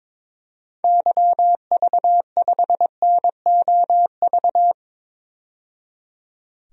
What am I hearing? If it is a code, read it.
Morse code, heard as a keyed tone, about 22 wpm, 700 Hz. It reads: YV5NOV
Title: DXCC-100 Call Signs - 6 Characters - 22wpm